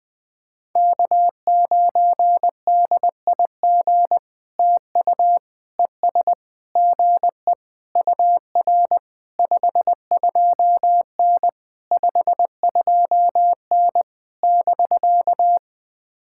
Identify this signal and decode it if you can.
Morse code, heard as a keyed tone, about 19 wpm, 700 Hz. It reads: K9DIG TU ES GE UR 52N 52N <BK>
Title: POTA QSO Practice - 20wpm